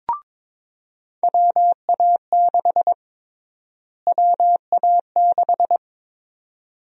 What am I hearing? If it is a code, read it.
Morse code, heard as a keyed tone, about 22 wpm, 700 Hz. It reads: WA6 WA6
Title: USA Call Sign Prefixes - Repeated 2x - 22wpm